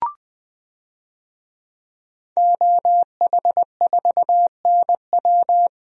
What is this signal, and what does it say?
Morse code, heard as a keyed tone, about 20 wpm, 700 Hz. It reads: OH4NW
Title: DXCC-100 Call Signs - 5 Characters - Repeated 3x - 20wpm